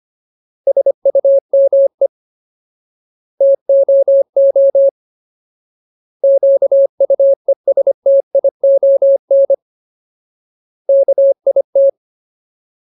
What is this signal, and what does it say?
Morse code, heard as a keyed tone, about 25 wpm, 550 Hz. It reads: SUME TOO QUESTION KIT
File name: Ultimate Word List / 25WPM Ultimate